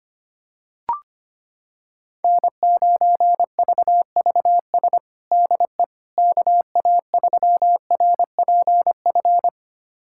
Morse code, heard as a keyed tone, about 25 wpm, 700 Hz. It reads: N9VVS DE KA3RPF
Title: USA Callsign DE Callsign 25wpm